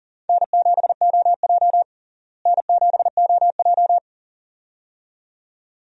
Morse code, heard as a keyed tone, about 37 wpm, 700 Hz. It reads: N7OJ N7OJ
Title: USA Call Signs - 4 Characters - Repeated 2x - 40wpm